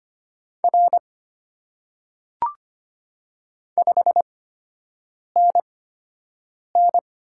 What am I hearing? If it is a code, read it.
Morse code, heard as a keyed tone, about 25 wpm, spaced as for 7 wpm, 690 Hz. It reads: R 5NN
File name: Top 90 QSO Elements - 8x Character Spacing - 25wpm